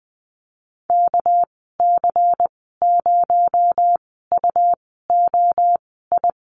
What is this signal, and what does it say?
Morse code, heard as a keyed tone, about 20 wpm, spaced as for 14 wpm, 700 Hz. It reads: KC0UOI
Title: 245 Lesson 39g - Letter z - Call Signs 20wpm